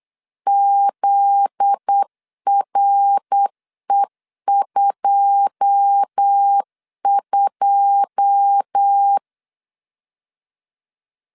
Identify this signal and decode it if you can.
Morse code, heard as a keyed tone, about 8 wpm, 800 Hz. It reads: ZRE22
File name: LEZIONI / 9 2 8wpm 3437